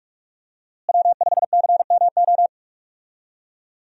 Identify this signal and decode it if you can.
Morse code, heard as a keyed tone, about 42 wpm, 700 Hz. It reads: W5CMO
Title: USA Call Signs - 5 Characters - Repeated 2x - 45wpm